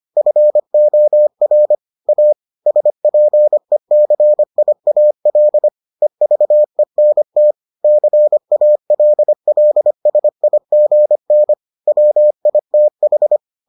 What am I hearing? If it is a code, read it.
Morse code, heard as a keyed tone, about 25 wpm, 600 Hz. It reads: FOR A SPECIAL EVENT CALLSIGN WITH